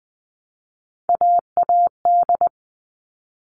Morse code, heard as a keyed tone, about 20 wpm, 700 Hz. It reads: AAD
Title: All Cut Numbers - 3-Digit 20wpm